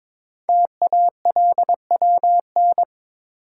Morse code, heard as a keyed tone, about 22 wpm, 700 Hz. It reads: TALWN
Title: Code Groups - 5 Letters - 22wpm